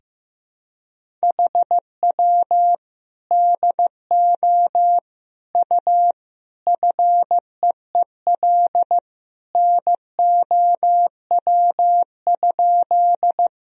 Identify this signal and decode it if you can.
Morse code, heard as a keyed tone, about 15 wpm, 700 Hz. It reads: HW DO U FEEL NOW?
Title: Sentences from Top 400 Words and Abbreviations 15wpm